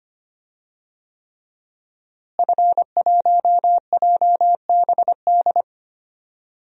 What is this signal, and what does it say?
Morse code, heard as a keyed tone, about 25 wpm, 700 Hz. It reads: F1JBD
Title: DXCC-100 Call Signs - 5 Characters - Repeated 3x - 25wpm